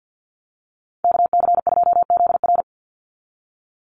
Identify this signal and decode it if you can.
Morse code, heard as a keyed tone, about 50 wpm, 700 Hz. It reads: KC2ZR
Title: USA Call Signs 50wpm